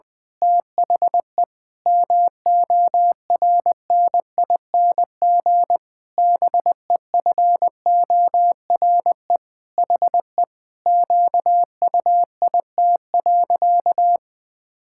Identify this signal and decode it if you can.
Morse code, heard as a keyed tone, about 20 wpm, 700 Hz. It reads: THE MORNING BEFORE HE QUIT.
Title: Pangram Challenge v2 - 20wpm